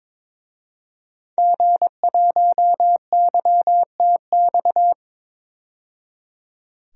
Morse code, heard as a keyed tone, about 22 wpm, 700 Hz. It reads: G1YTX
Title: DXCC-100 Call Signs - 5 Characters - 22wpm